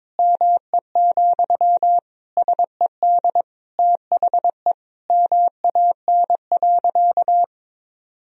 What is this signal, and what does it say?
Morse code, heard as a keyed tone, about 23 wpm, 700 Hz. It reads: ME, SED THE MAN.
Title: Sentences from Top 200 Words and Abbreviations 22wpm